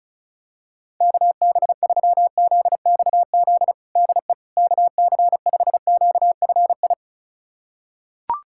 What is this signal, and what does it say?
Morse code, heard as a keyed tone, about 34 wpm, 700 Hz. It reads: KB3ZXZ DE KC5QFI
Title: USA Callsign DE Callsign - 6 Characters - 35wpm